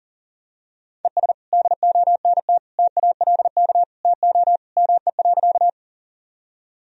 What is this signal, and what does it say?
Morse code, heard as a keyed tone, about 40 wpm, 700 Hz. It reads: ES DONT TALK TO ME.
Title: Sentences from Top 400 Words and Abbreviations 40wpm